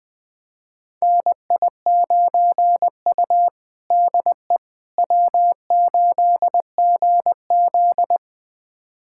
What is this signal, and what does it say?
Morse code, heard as a keyed tone, about 21 wpm, 700 Hz. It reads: NI9U DE W8GZ
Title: USA Callsign DE Callsign - 4 Characters - 20wpm